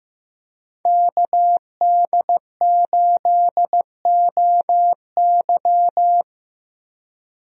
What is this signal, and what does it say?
Morse code, heard as a keyed tone, about 15 wpm, 700 Hz. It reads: KD8OY
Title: USA Call Signs - 5 Characters - Repeated 3x - 15wpm